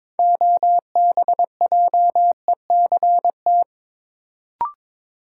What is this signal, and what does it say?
Morse code, heard as a keyed tone, about 22 wpm, 700 Hz. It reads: OBJECT
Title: Sentences from Top 500 Words - New Word Drill - 22wpm